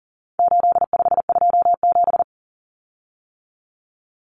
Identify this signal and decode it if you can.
Morse code, heard as a keyed tone, about 40 wpm, 700 Hz. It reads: 8527
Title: Common Cut Numbers - 4-Digit 40wpm